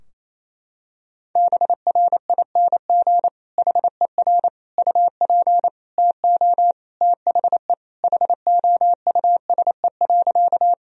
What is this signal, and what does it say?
Morse code, heard as a keyed tone, about 28 wpm, 700 Hz. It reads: BRING HER UP TO THE HOUSE.
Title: Sentences from Top 1000 Words - Review Part 3 - 28wpm